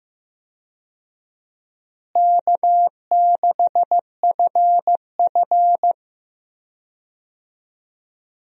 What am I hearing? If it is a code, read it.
Morse code, heard as a keyed tone, about 15 wpm, 700 Hz. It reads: K6FF
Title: USA Call Signs - 4 Characters - Repeated 2x - 15wpm